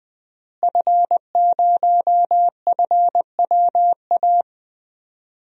Morse code, heard as a keyed tone, about 20 wpm, 700 Hz. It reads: F0FWA
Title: DXCC-100 Call Signs - 5 Characters - Repeated 2x - 20wpm